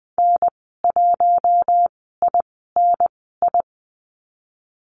Morse code, heard as a keyed tone, about 20 wpm, spaced as for 15 wpm, 700 Hz. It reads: N1INI
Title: 048 Lesson 9g - Number 4 - Call Signs 20wpm